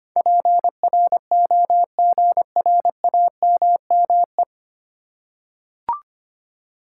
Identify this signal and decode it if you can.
Morse code, heard as a keyed tone, about 25 wpm, 700 Hz. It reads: PROGRAMME
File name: 9 Letter Words - First 3 Letters Given - 25wpm